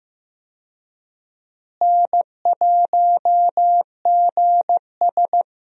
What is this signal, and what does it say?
Morse code, heard as a keyed tone, about 15 wpm, 700 Hz. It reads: N1GS
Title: USA Call Signs - 4 Characters - Repeated 2x - 15wpm